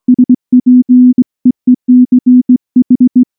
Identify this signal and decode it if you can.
Morse code, heard as a keyed tone, about 19 wpm, 250 Hz. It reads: SPEECH
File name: morse / morse-training-20WPM-1